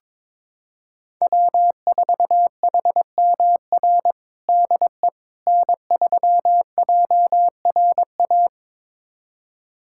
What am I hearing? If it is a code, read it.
Morse code, heard as a keyed tone, about 22 wpm, 700 Hz. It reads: W4HMR DE N3JRA
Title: USA Callsign DE Callsign - 5 Characters - 22wpm